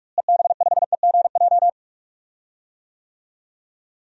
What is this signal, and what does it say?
Morse code, heard as a keyed tone, about 45 wpm, 690 Hz. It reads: EB5EGJ
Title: DXCC-100 Call Signs - 6 Characters - Repeated 2x - 45wpm